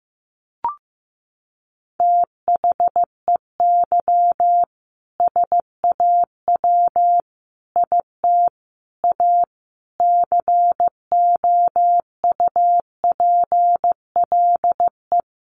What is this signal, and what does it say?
Morse code, heard as a keyed tone, about 15 wpm, 700 Hz. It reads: THEY SAW IT A COUPLE
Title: Sentences from Top 1100 Words 15wpm